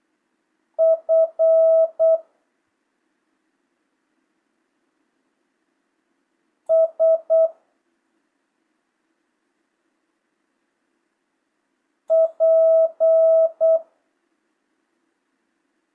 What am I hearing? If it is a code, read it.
Morse code, heard as a keyed tone, about 8 wpm, 650 Hz. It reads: F S P
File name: jh8cbh / 2025cw